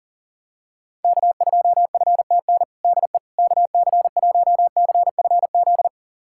Morse code, heard as a keyed tone, about 40 wpm, 700 Hz. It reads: K2FTN DE KC1CFZ
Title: USA Callsign DE Callsign 40wpm